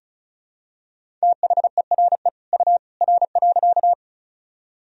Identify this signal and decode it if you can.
Morse code, heard as a keyed tone, about 36 wpm, 700 Hz. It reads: THERE U R.
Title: Sentences from Top 100 Words and Abbreviations 35wpm